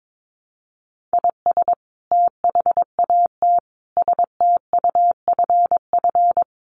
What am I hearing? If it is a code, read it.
Morse code, heard as a keyed tone, about 22 wpm, 700 Hz. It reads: IS THAT STUFF
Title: Sentences from Top 1400 Words 22wpm